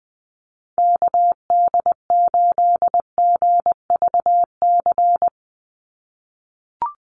Morse code, heard as a keyed tone, about 20 wpm, 700 Hz. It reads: KD8GVC
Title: USA Call Signs 20wpm